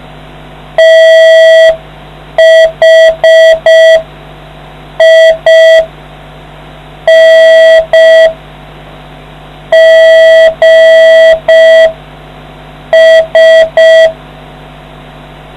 Morse code, heard as a keyed tone, about 5 wpm, spaced as for 4 wpm, 650 Hz. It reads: THINGS